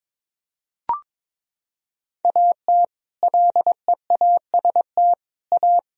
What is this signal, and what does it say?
Morse code, heard as a keyed tone, about 22 wpm, 700 Hz. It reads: AT LEAST A
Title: Sentences from Top 1000 Words - Review Part 2 - 22wpm